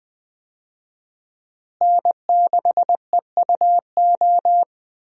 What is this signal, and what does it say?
Morse code, heard as a keyed tone, about 20 wpm, 700 Hz. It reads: N6EUO
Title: USA Call Signs - 5 Characters - Repeated 3x - 20wpm